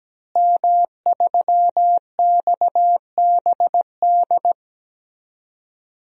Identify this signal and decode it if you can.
Morse code, heard as a keyed tone, about 17 wpm, 700 Hz. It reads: M3XBD
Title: DXCC-100 Call Signs - 5 Characters - Repeated 3x - 17wpm